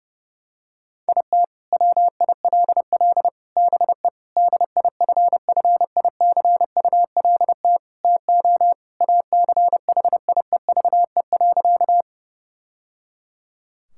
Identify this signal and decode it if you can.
Morse code, heard as a keyed tone, about 30 wpm, 700 Hz. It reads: IT WILL BE DIFFICULT TO ACHIEVE.
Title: Sentences from Top 1000 Words - Review Part 6 - 30wpm